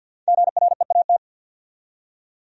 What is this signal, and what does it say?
Morse code, heard as a keyed tone, about 67 wpm, spaced as for 58 wpm, 700 Hz. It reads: GREAT
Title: Speed-Racing Top 300 Words 50wpm